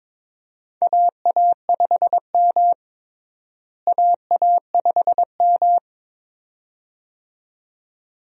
Morse code, heard as a keyed tone, about 22 wpm, 700 Hz. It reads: AA5M AA5M
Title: USA Call Signs - 4 Characters - Repeated 2x - 22wpm